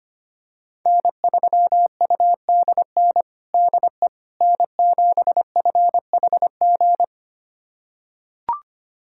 Morse code, heard as a keyed tone, about 25 wpm, 700 Hz. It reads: N3UDN DE N7FHG
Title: USA Callsign DE Callsign - 5 Characters - 25wpm